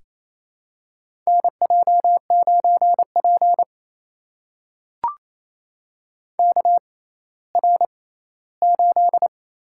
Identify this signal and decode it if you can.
Morse code, heard as a keyed tone, about 28 wpm, spaced as for 16 wpm, 700 Hz. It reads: NJ9P KR8
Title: USA Call Signs - 4 Characters - 6x Character Spacing - 28wpm